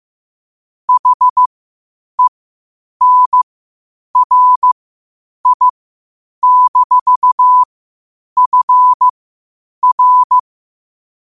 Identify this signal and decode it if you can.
Morse code, heard as a keyed tone, about 15 wpm, spaced as for 8 wpm, 1000 Hz. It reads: HENRI-FR
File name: qotdfiles / 8wpm